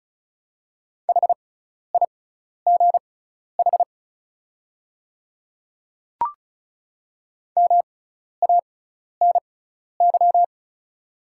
Morse code, heard as a keyed tone, about 34 wpm, spaced as for 12 wpm, 700 Hz. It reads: HIGH MANY
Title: 6x Character Spacing - Top 500 Words - 35wpm